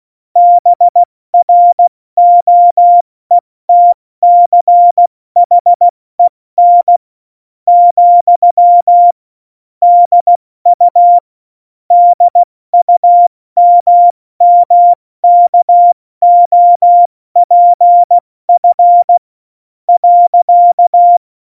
Morse code, heard as a keyed tone, about 16 wpm, 700 Hz. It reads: BROETCHEN , DU DUMMKOPF .
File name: Morsekurs / Witze / Witz 11. September 2018, 70 F80